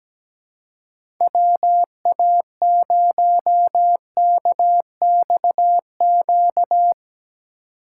Morse code, heard as a keyed tone, about 17 wpm, 700 Hz. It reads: WA0KXQ
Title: USA Call Signs - 6 Characters - 3x Character Spacing - 17wpm